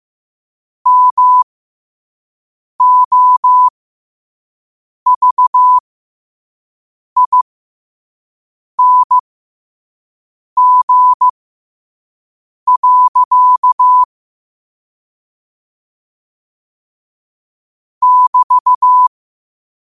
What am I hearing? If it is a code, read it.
Morse code, heard as a keyed tone, about 15 wpm, spaced as for 5 wpm, 1000 Hz. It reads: MOVING. <BT>
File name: 5wpm